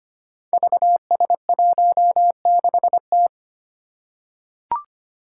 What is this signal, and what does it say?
Morse code, heard as a keyed tone, about 25 wpm, 700 Hz. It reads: VS16T
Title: Five Challenging Letter-Number Combinations 25wpm